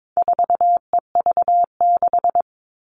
Morse code, heard as a keyed tone, about 22 wpm, 700 Hz. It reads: 4EV6
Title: All Cut Numbers - 4-Digit 22wpm